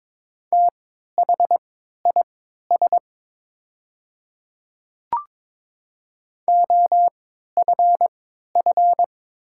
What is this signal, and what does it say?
Morse code, heard as a keyed tone, about 22 wpm, spaced as for 13 wpm, 690 Hz. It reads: THIS OFF
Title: 3x Character Spacing - Top 200 Words - 22wpm